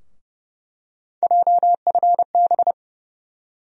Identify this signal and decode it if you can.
Morse code, heard as a keyed tone, about 32 wpm, 700 Hz. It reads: JFB
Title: USA Call Sign Postfixes - 3 Letters 30wpm